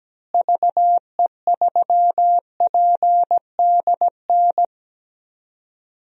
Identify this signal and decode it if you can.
Morse code, heard as a keyed tone, about 17 wpm, 700 Hz. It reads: VE3PDN
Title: DXCC-100 Call Signs - 6 Characters - Repeated 3x - 17wpm